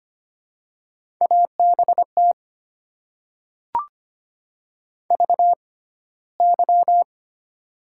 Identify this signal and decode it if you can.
Morse code, heard as a keyed tone, about 25 wpm, spaced as for 15 wpm, 700 Hz. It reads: ABT VY
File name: Top 90 QSO Elements - 6x Character Spacing - 25wpm